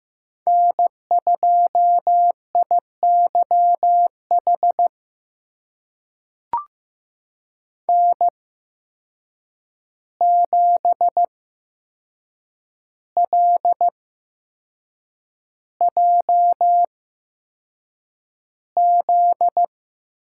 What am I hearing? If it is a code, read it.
Morse code, heard as a keyed tone, about 15 wpm, spaced as for 6 wpm, 700 Hz. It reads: N2IYH N7LJZ
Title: USA Call Signs - 5 Characters - 8x Character Spacing - 15wpm